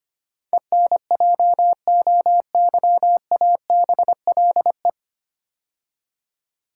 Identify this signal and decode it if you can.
Morse code, heard as a keyed tone, about 25 wpm, 700 Hz. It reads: ENJOYABLE
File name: practice / 25wpm / 9 Letter Words - First 3 Letters Given - 25wpm